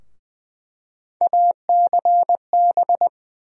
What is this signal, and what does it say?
Morse code, heard as a keyed tone, about 20 wpm, 700 Hz. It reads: ACB
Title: USA Call Sign Postfixes - 3 Letters 20wpm